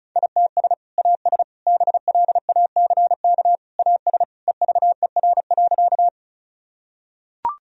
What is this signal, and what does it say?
Morse code, heard as a keyed tone, about 36 wpm, 700 Hz. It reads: ITS AS BLACK AS EVER.
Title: Sentences from Top 1000 Words - Review Part 2 - 35wpm